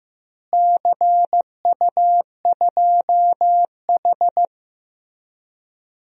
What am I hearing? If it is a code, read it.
Morse code, heard as a keyed tone, about 15 wpm, 700 Hz. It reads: CU2H
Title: DXCC-100 Call Signs - 4 Characters - Repeated 3x - 15wpm